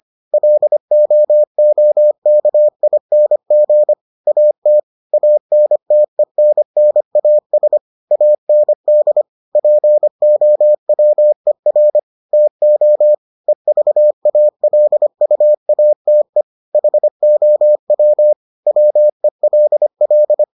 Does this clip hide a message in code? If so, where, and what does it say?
Yes: Morse code from 0.3 to 20.4 s, heard as a keyed tone, about 25 wpm, 600 Hz. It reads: LOOKING AT ANTENNAS AND POWER TO EVALUATE HOW WELL